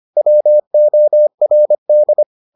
Morse code, heard as a keyed tone, about 25 wpm, 600 Hz. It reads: WORD